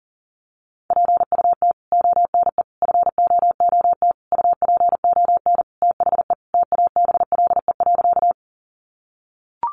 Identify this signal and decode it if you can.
Morse code, heard as a keyed tone, about 40 wpm, 700 Hz. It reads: PUT ONE FOOT UPON THE TABLE.
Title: Sentences from Top 1100 Words 40wpm